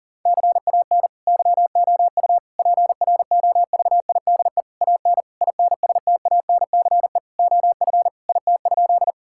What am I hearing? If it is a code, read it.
Morse code, heard as a keyed tone, about 40 wpm, 700 Hz. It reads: CAN YOU PROVIDE AN INSTANCE OF IT?
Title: Sentences from Top 1000 Words - Review Part 5 - 40wpm